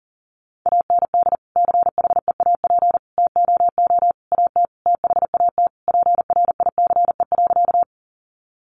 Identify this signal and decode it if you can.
Morse code, heard as a keyed tone, about 40 wpm, 700 Hz. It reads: AND CHEAP TOO AT THAT PRICE.
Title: Sentences from Top 1200 Words 40wpm